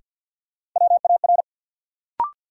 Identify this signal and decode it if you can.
Morse code, heard as a keyed tone, about 51 wpm, 700 Hz. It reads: WAR
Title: Sets of 5 Words - New Word Drill - 50wpm